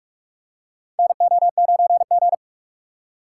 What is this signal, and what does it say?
Morse code, heard as a keyed tone, about 43 wpm, 700 Hz. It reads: NO9G
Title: USA Call Signs - 4x Character Spacing - 45wpm